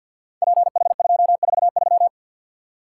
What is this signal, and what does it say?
Morse code, heard as a keyed tone, about 53 wpm, 700 Hz. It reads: PH243
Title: Code Groups - 5 Letter:Numbers - 50wpm